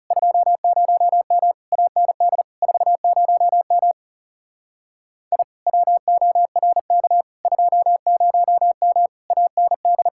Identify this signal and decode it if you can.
Morse code, heard as a keyed tone, about 37 wpm, 700 Hz. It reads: 20M AND 40M I WORK 20M AND
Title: Speed-Racing Common QSO Phrases Encore - 30wpm